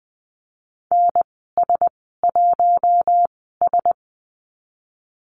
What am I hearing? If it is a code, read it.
Morse code, heard as a keyed tone, about 20 wpm, spaced as for 15 wpm, 700 Hz. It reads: NS1S
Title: 140 Lesson 23g - Letter f - Call Signs 20wpm